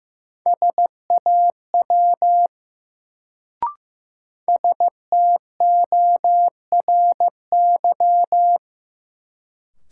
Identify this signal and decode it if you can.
Morse code, heard as a keyed tone, about 15 wpm, 700 Hz. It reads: SAW STORY
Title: Sentences from Top 300 Words - New Word Drill - 15wpm